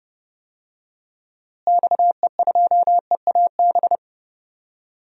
Morse code, heard as a keyed tone, about 30 wpm, 700 Hz. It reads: XE2EAB
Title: DXCC-100 Call Signs - 6 Characters - Repeated 3x - 30wpm